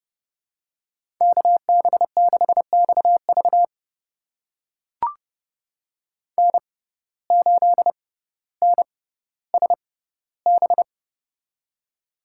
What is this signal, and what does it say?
Morse code, heard as a keyed tone, about 30 wpm, spaced as for 15 wpm, 700 Hz. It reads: KB6XV N8NSB
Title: USA Call Signs - 5 Characters - 6x Character Spacing - 30wpm